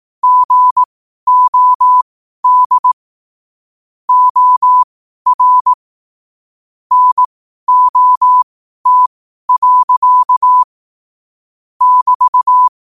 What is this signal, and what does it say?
Morse code, heard as a keyed tone, about 18 wpm, spaced as for 12 wpm, 1000 Hz. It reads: GOD OR NOT. <BT>
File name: qotdfiles / 12wpm